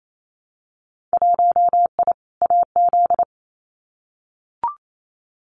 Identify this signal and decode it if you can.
Morse code, heard as a keyed tone, about 28 wpm, 700 Hz. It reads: 1I AZ
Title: Winter Field Day Exchanges 28wpm